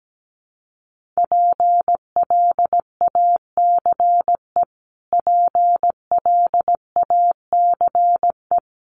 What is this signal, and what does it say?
Morse code, heard as a keyed tone, about 17 wpm, 700 Hz. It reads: PLACE PLACE
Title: Top 200 Words Repeated 17wpm